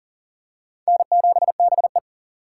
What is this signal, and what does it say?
Morse code, heard as a keyed tone, about 40 wpm, 700 Hz. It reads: N7BE
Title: USA Call Signs - 4 Characters - 6x Character Spacing - 40wpm